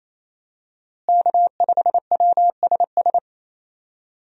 Morse code, heard as a keyed tone, about 28 wpm, 700 Hz. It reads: K5WSS
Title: USA Call Signs - 5 Characters - Repeated 2x - 28wpm